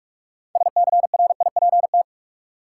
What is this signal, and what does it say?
Morse code, heard as a keyed tone, about 42 wpm, 700 Hz. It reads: SCRIPT
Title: Six Letter Words - 45wpm